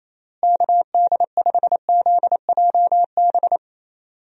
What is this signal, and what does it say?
Morse code, heard as a keyed tone, about 27 wpm, 700 Hz. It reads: KD5ZJB
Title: USA Call Signs - 6 Characters - Repeated 3x - 28wpm